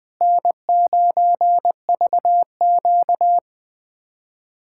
Morse code, heard as a keyed tone, about 21 wpm, 700 Hz. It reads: N9VQ
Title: USA Callsign DE Callsign - 4 Characters - 20wpm